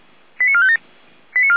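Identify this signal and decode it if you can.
Five-tone selective call (ZVEI), heard as quick runs of five tones, tones of about 70 ms. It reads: 89457
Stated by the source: ZVEI five-tone selective call received on an SDR, from the Feuerwehr Orlamünde audio server.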